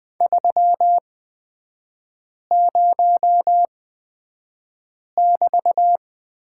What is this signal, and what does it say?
Morse code, heard as a keyed tone, about 20 wpm, spaced as for 5 wpm, 700 Hz. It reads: 30<BT>
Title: 224 Lesson 36e - BT - Three Characters 20wpm